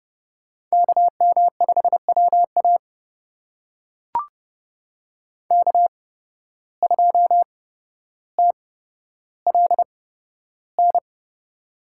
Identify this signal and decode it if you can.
Morse code, heard as a keyed tone, about 30 wpm, spaced as for 13 wpm, 700 Hz. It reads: KM5WA K2TLN
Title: USA Call Signs - 8x Character Spacing - 30wpm